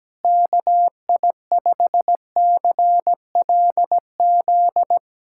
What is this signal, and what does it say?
Morse code, heard as a keyed tone, about 17 wpm, 700 Hz. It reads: KI5CLZ
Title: USA Callsign DE Callsign - 6 Characters - 17wpm